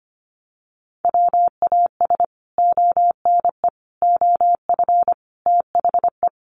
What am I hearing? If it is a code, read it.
Morse code, heard as a keyed tone, about 25 wpm, 700 Hz. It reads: WAS ONE OF THE
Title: Sets of 5 Words Encore 25wpm